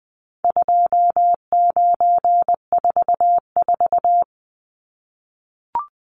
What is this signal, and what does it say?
Morse code, heard as a keyed tone, about 19 wpm, 700 Hz. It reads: 2944
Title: Numbers - 4-Digit 20wpm